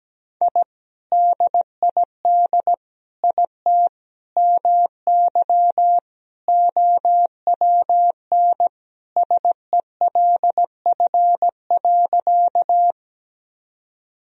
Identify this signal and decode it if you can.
Morse code, heard as a keyed tone, about 17 wpm, 700 Hz. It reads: I DID IT MY OWN SELF.
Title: Sentences from Top 1000 Words - Review Part 1 - 17wpm